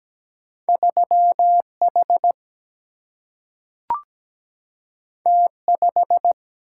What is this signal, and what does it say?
Morse code, heard as a keyed tone, about 17 wpm, 700 Hz. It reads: 3H T5
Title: Single-Letter Single-Number 17wpm